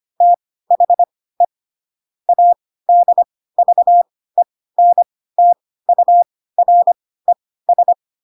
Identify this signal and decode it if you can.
Morse code, heard as a keyed tone, about 25 wpm, spaced as for 16 wpm, 700 Hz. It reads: THE ADVENTURES
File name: Pinocchio / Pinocchio ch5 10 25 ik1hge